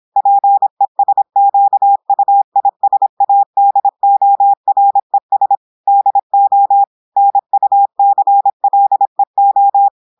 Morse code, heard as a keyed tone, about 26 wpm, 800 Hz. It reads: PESQUISADORES DO NUCLEO
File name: audios / portugais / portugais texte en clair 1 vitesse 26 mots minutes